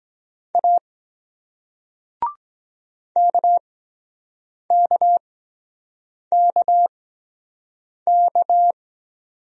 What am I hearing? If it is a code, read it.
Morse code, heard as a keyed tone, about 23 wpm, spaced as for 7 wpm, 700 Hz. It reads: A KKKK